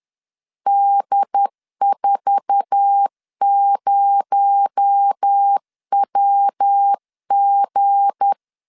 Morse code, heard as a keyed tone, about 10 wpm, 800 Hz. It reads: D40WG
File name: CW / DET / 1 9 1500C S B 10wpm 4439